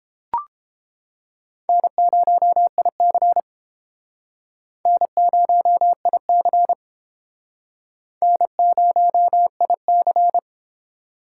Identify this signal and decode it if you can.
Morse code, heard as a keyed tone, about 30 wpm, 700 Hz. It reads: N0IC N0IC N0IC